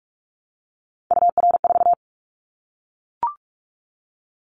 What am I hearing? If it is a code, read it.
Morse code, heard as a keyed tone, about 44 wpm, 700 Hz. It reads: UR4
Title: Callsign Prefixes for Top 25 DXCC - Repeated 3x - 45wpm